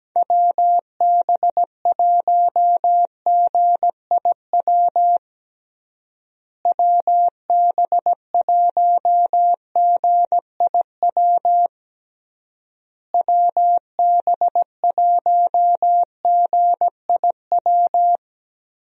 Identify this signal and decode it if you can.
Morse code, heard as a keyed tone, about 17 wpm, 700 Hz. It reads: WB1GIW WB1GIW WB1GIW
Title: USA Call Signs - 6 Characters - Repeated 3x - 17wpm